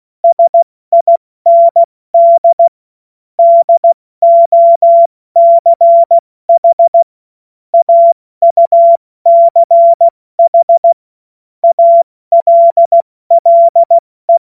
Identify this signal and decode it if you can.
Morse code, heard as a keyed tone, about 16 wpm, 680 Hz. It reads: SIND DOCH AUCH ALLE
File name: Morsekurs / Witze / Witz 18. Januar 2022, 70 F80